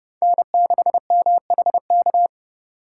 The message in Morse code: N6MHK

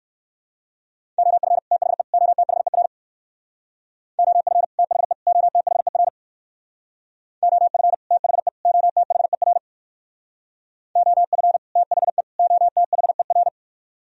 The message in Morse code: OF THE OTHER OF THE OTHER OF THE OTHER OF THE OTHER

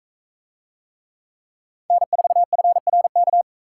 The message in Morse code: N4FRK

